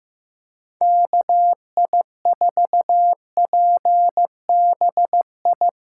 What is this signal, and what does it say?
Morse code, heard as a keyed tone, about 15 wpm, 700 Hz. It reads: KI4PBI
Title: USA Callsign DE Callsign 15wpm